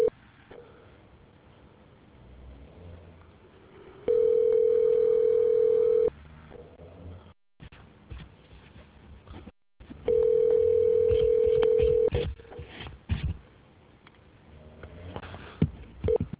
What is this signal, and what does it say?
Ringback tone (North America), 440 + 480 Hz.